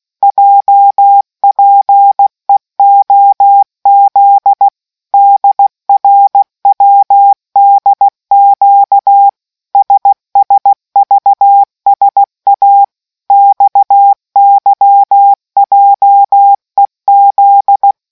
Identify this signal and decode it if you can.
Morse code, heard as a keyed tone, about 16 wpm, 800 Hz. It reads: JPEOZ DRWDQ SSVSA XYJEZ